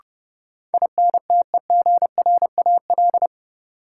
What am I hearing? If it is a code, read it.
Morse code, heard as a keyed tone, about 30 wpm, 700 Hz. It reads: INTEGRAL
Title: Sets of 5 Words - New Word Drill - 30wpm